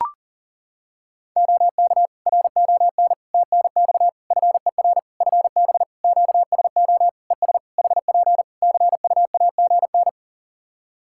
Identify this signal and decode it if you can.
Morse code, heard as a keyed tone, about 40 wpm, 700 Hz. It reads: OK RON TNX FER FB QSO ES HP CUAGN
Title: Practice QSO - Exchange 4 - Ending - 40wpm